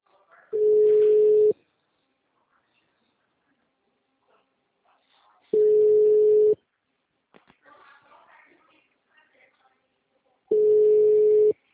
Ringback tone (Europe, 425 Hz).